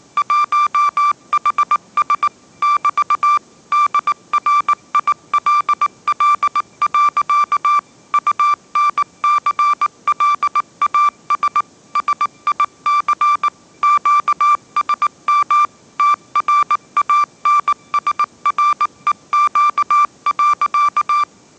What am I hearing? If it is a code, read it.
Morse code, heard as a keyed tone, about 19 wpm, 1180 Hz. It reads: 1HS <BT> DRILL. UNCLAS SIC QSM TRANSREQ.
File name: mrx083 20wpm